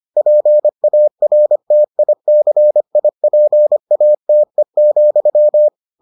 Morse code, heard as a keyed tone, about 25 wpm, 600 Hz. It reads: PARTICIPATE,